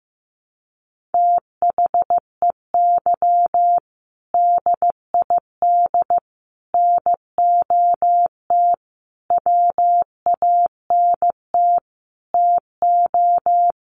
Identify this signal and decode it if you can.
Morse code, heard as a keyed tone, about 15 wpm, 700 Hz. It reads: THEY DID NOT WANT TO
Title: Sets of 5 Words 15wpm